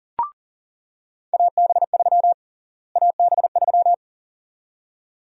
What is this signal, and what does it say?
Morse code, heard as a keyed tone, about 40 wpm, 700 Hz. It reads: AB3 AB3
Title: USA Call Sign Prefixes - Repeated 2x - 40wpm